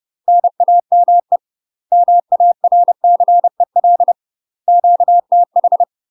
Morse code, heard as a keyed tone, about 30 wpm, 700 Hz. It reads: NAME MARCEL QTH
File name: UK MASTER QSOs TRAINING / 30WPM / UK MASTER QSOs 0023